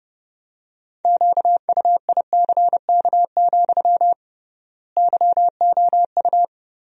Morse code, heard as a keyed tone, about 30 wpm, 700 Hz. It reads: QUICK, YOU
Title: 3x Word-Spacing - Sentences from Top 500 Words - 30wpm